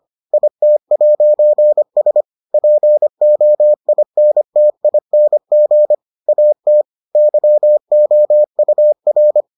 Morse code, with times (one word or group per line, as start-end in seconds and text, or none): 0.3-2.2 IT'S
2.5-5.9 POINTING
6.3-6.8 AT
7.2-9.4 YOUR